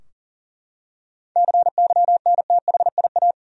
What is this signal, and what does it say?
Morse code, heard as a keyed tone, about 40 wpm, 700 Hz. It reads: CYNTHIA
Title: USA Common Names - 40wpm